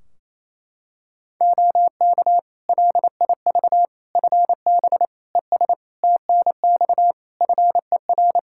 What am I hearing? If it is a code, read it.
Morse code, heard as a keyed tone, about 28 wpm, 700 Hz. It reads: OK LIV FB ES TNX FER
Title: Practice QSO - Exchange 2 - Info - 28wpm